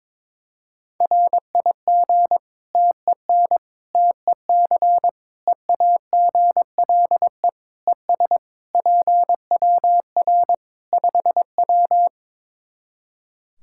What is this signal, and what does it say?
Morse code, heard as a keyed tone, about 22 wpm, 700 Hz. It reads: RIG TEN TEC EAGLE ES PWR 5W
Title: Practice QSO - Exchange 2 - Info - 22wpm